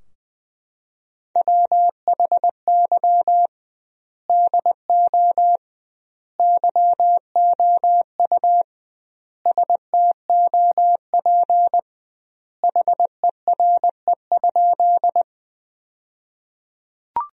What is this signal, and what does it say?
Morse code, heard as a keyed tone, about 20 wpm, 700 Hz. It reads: WHY DO YOU STOP HERE?
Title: 2x Word-Spacing - Sentences from Top 300 Words - 20wpm